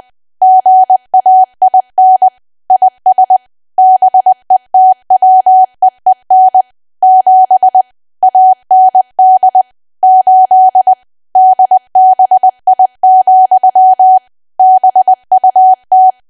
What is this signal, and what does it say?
Morse code, heard as a keyed tone, about 20 wpm, 750 Hz. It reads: GAIN IS BETWEEN 7 AND 8 DBI, BUT
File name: cwa / W1AW-Level3 / 20WPM